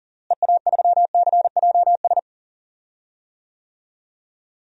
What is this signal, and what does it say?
Morse code, heard as a keyed tone, about 40 wpm, 700 Hz. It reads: EA3CJS
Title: DXCC-100 Call Signs - 6 Characters - Repeated 2x - 40wpm